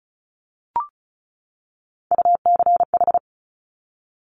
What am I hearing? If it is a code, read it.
Morse code, heard as a keyed tone, about 36 wpm, 700 Hz. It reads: UCH
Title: Top Three Letter Combinations in Words 35wpm